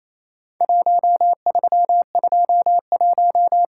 Morse code, heard as a keyed tone, about 28 wpm, 700 Hz. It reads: 1321
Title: Worked All Europe DX Contest - QTC Exchange - 28wpm